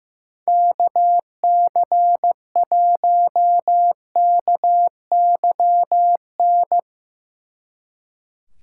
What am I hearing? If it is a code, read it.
Morse code, heard as a keyed tone, about 17 wpm, 700 Hz. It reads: KC1KYN
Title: Speed Racing - USA 2x3 Call Signs 15wpm